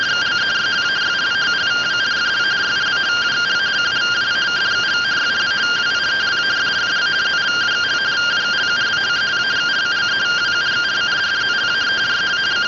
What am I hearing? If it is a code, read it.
Two-tone teleprinter signal, about 170 Hz shift.
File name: findings / axial iodine 10mm 184.gen-31 rtty rx